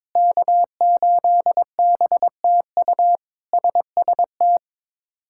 Morse code, heard as a keyed tone, about 22 wpm, 700 Hz. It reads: K8BTU SST
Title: K1USN SST QSO Practice - 22wpm